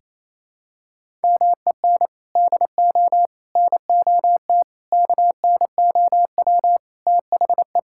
Morse code, heard as a keyed tone, about 28 wpm, 700 Hz. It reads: MEN DO NOT KNOW THE
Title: Sentences from Top 1000 Words - Review Part 6 - 28wpm